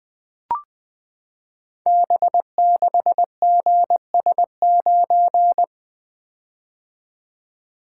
Morse code, heard as a keyed tone, about 20 wpm, 700 Hz. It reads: B6GS9
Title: Code Groups - 5 Letter:Numbers - 20wpm